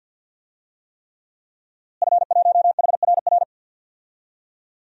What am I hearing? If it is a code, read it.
Morse code, heard as a keyed tone, about 52 wpm, 700 Hz. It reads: F1HRR
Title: DXCC-100 Call Signs - 5 Characters - Repeated 3x - 50wpm